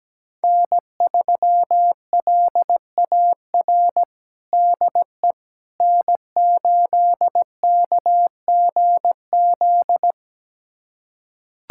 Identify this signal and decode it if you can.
Morse code, heard as a keyed tone, about 17 wpm, 700 Hz. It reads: N3LAR DE N8KGZ
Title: USA Callsign DE Callsign - 5 Characters - 17wpm